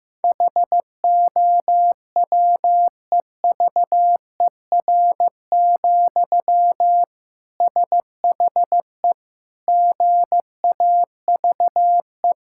Morse code, heard as a keyed tone, about 16 wpm, 700 Hz. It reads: HOWEVER, SHE GAVE
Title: Sentences from Top 1000 Words - Review Part 6 - 15wpm